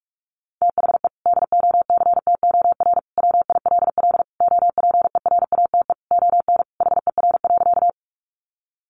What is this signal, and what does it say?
Morse code, heard as a keyed tone, about 44 wpm, 700 Hz. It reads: THE DOCTOR WILL OPERATE ON HER.
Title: Sentences from Top 1500 Words 45wpm